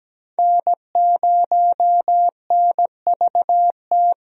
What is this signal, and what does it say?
Morse code, heard as a keyed tone, about 17 wpm, 700 Hz. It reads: N0NVT
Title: USA Call Signs - 5 Characters - 17wpm